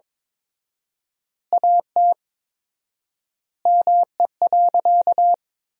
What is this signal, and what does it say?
Morse code, heard as a keyed tone, about 22 wpm, 700 Hz. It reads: AT ME.
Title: 4x Word-Spacing - Sentences from Top 200 Words - 22wpm